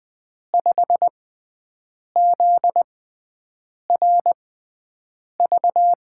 Morse code, heard as a keyed tone, about 20 wpm, spaced as for 7 wpm, 700 Hz. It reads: 5ZRV
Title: USA Call Signs - 6 Characters - 6x Character Spacing - 20wpm